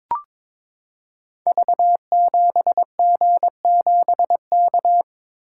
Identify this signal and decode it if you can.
Morse code, heard as a keyed tone, about 22 wpm, 700 Hz. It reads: V7G7K
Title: Code Groups - 5 Letter:Numbers - 22wpm